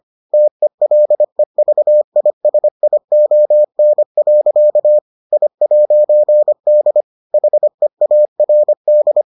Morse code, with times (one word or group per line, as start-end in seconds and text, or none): 0.3-5.0 TELEVISION.
5.3-7.0 I'D
7.3-9.2 HEARD